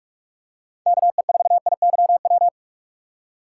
Morse code, heard as a keyed tone, about 44 wpm, 700 Hz. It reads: KE4IYW
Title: USA Call Signs - 6 Characters - 4x Character Spacing - 45wpm